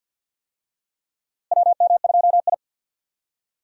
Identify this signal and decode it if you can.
Morse code, heard as a keyed tone, about 52 wpm, 700 Hz. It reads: WM2I
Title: USA Call Signs - 4 Characters - Repeated 2x - 50wpm